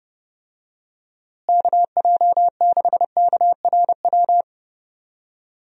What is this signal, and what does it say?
Morse code, heard as a keyed tone, about 29 wpm, 700 Hz. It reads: KJ6KRW
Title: USA Call Signs - 6 Characters - Repeated 3x - 30wpm